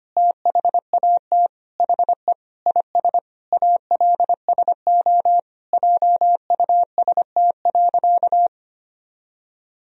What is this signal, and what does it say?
Morse code, heard as a keyed tone, about 25 wpm, 700 Hz. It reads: THAT HE IS ALSO JUST.
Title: Sentences from Top 1000 Words - Review Part 1 - 25wpm